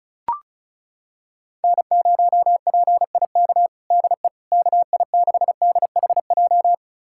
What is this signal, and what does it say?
Morse code, heard as a keyed tone, about 34 wpm, 700 Hz. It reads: N0PIK DE KI6DHJ
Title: USA Callsign DE Callsign 35wpm